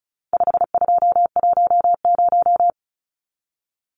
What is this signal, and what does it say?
Morse code, heard as a keyed tone, about 34 wpm, 700 Hz. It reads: 5210
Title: Numbers - 4-Digit 35wpm